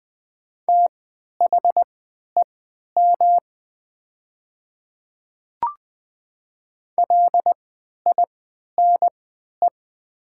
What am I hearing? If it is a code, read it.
Morse code, heard as a keyed tone, about 20 wpm, spaced as for 11 wpm, 700 Hz. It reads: THEM LINE
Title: 3x Character Spacing - Top 200 Words - 20wpm